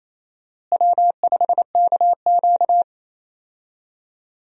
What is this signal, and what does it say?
Morse code, heard as a keyed tone, about 28 wpm, 700 Hz. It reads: W5KQ
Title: USA Call Signs - 4 Characters - Repeated 3x - 28wpm